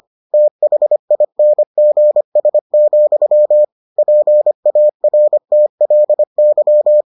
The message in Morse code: THINGS, PARTLY

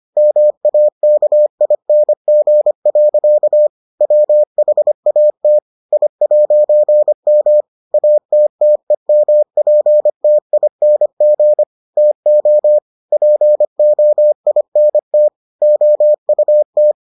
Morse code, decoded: MAKING. WHAT I'M ATTEMPTING TO POINT OUT

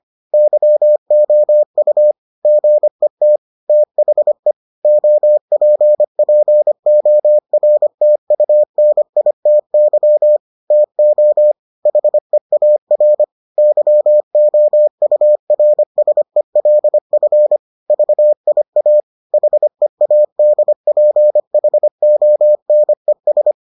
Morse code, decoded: YOU GET THE OPPORTUNITY TO HEAR YOURSELF VIA HEADPHONES

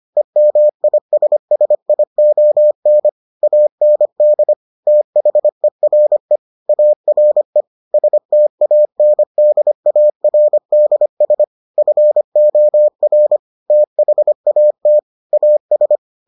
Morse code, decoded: EMISSION AND THERE ARE STANDARDS FOR THAT AS

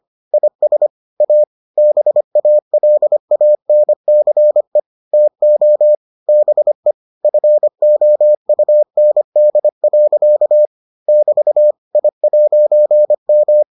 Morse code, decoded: IS A BALANCE TO BE FOUND. <BT> I'M

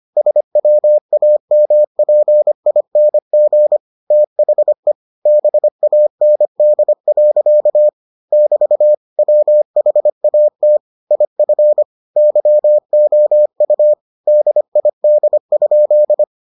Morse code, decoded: SWAMPING THE BAND. <BT> WHAT IF YOU DID?

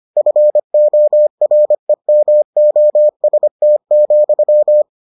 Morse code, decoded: FOREMOST,

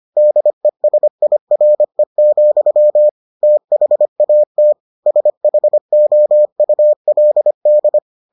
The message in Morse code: DESIRE, THAT SHOULD

